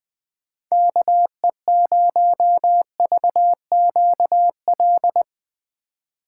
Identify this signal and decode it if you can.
Morse code, heard as a keyed tone, about 20 wpm, 700 Hz. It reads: KE0VQL
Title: USA Call Signs - 3x Character Spacing - 20wpm